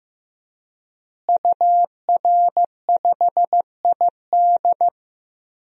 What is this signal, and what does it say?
Morse code, heard as a keyed tone, about 15 wpm, 700 Hz. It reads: UR5ID
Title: DXCC-100 Call Signs - 5 Characters - Repeated 3x - 15wpm